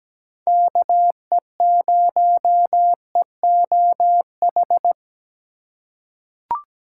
Morse code, heard as a keyed tone, about 17 wpm, 700 Hz. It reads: KE0EOH
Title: USA Call Signs - 6 Characters - 3x Character Spacing - 17wpm